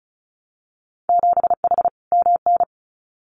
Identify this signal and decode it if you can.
Morse code, heard as a keyed tone, about 34 wpm, 700 Hz. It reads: 7H MN
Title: Winter Field Day Exchanges Encore 35wpm